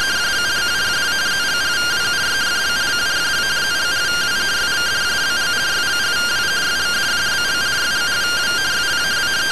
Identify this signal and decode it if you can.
Two-tone teleprinter signal, about 170 Hz shift.